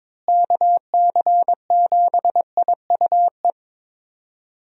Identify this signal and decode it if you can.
Morse code, heard as a keyed tone, about 22 wpm, 700 Hz. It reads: KC7IUE
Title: USA Call Signs - 6 Characters - Repeated 3x - 22wpm